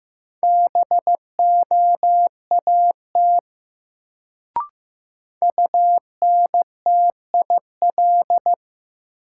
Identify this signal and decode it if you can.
Morse code, heard as a keyed tone, about 16 wpm, 700 Hz. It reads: BOAT UNTIL
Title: Sentences from Top 500 Words - New Word Drill - 15wpm